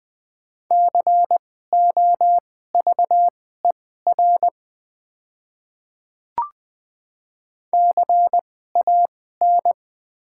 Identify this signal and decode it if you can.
Morse code, heard as a keyed tone, about 20 wpm, spaced as for 14 wpm, 700 Hz. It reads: COVER CAN
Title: 2x Character Spacing - Top 500 Words - 20wpm